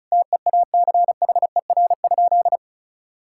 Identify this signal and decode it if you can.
Morse code, heard as a keyed tone, about 36 wpm, 700 Hz. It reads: TEACHER?
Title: Sentences from Top 1000 Words - Review Part 6 - 35wpm